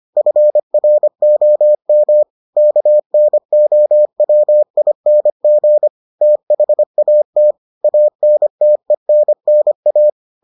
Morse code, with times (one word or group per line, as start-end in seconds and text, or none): 0.2-2.2 FROM
2.6-5.9 KNOWING
6.2-7.5 THAT
7.8-10.1 ANTENNA